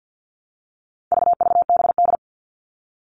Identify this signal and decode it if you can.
Morse code, heard as a keyed tone, about 50 wpm, 700 Hz. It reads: VVBD
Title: All Cut Numbers - 4-Digit 50wpm